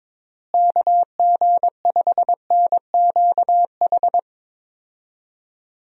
Morse code, heard as a keyed tone, about 22 wpm, 700 Hz. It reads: KG5NQH
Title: USA Call Signs - 6 Characters - Repeated 2x - 22wpm